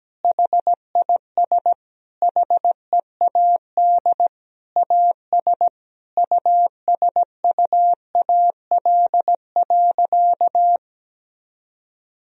Morse code, heard as a keyed tone, about 17 wpm, 700 Hz. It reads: HIS HEAD AS USUAL.
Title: Sentences from Top 1000 Words - Review Part 2 - 17wpm